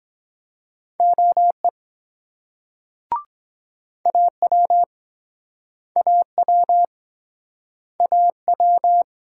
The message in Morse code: OE AW AW AW

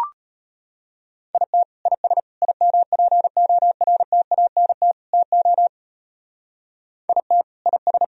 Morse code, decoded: IT IS IMPORTANT TO IT IS

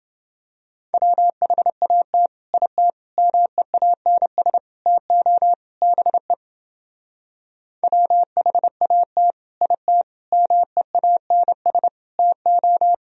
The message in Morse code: WHAT IT MEANS TO BE WHAT IT MEANS TO

